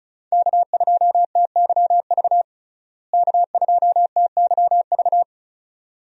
Morse code, heard as a keyed tone, about 34 wpm, 700 Hz. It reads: K2TYV K2TYV
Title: USA Call Signs - 5 Characters - Repeated 2x - 35wpm